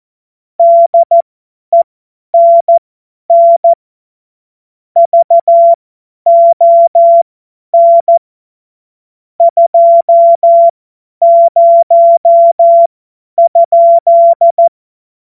Morse code, heard as a keyed tone, about 14 wpm, spaced as for 10 wpm, 680 Hz. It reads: DENN VON 20?
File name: Morsekurs / Witze / Witz 14. März 2023 50 F70